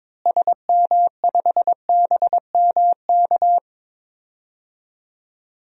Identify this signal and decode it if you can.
Morse code, heard as a keyed tone, about 21 wpm, 700 Hz. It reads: SM5BMK
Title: DXCC-100 Call Signs - 6 Characters - Repeated 2x - 22wpm